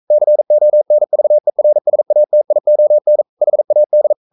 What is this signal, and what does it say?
Morse code, heard as a keyed tone, about 42 wpm, 600 Hz. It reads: CONVERSATION HAD
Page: Romanen Never af Ken Follet på engelsk.